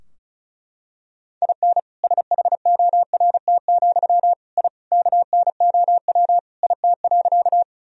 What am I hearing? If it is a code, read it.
Morse code, heard as a keyed tone, about 36 wpm, 700 Hz. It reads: IN SHORT, I KNOW IT.
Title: Sentences from Top 1000 Words - Review Part 2 - 35wpm